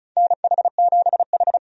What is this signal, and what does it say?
Morse code, heard as a keyed tone, about 36 wpm, 700 Hz. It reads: NH7H
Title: USA Call Signs - 4 Characters - 35wpm